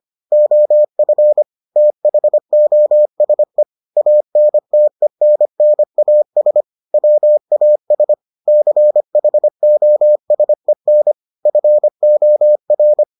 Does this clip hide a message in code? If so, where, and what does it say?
Yes: Morse code from 0.3 to 13.0 s, heard as a keyed tone, about 25 wpm, 600 Hz. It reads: OF THOSE ANTENNAS WAS CHOSEN FOR